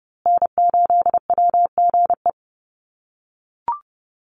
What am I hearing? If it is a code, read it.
Morse code, heard as a keyed tone, about 30 wpm, 700 Hz. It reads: N8WGE
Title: USA Call Signs Repeated 3x 30wpm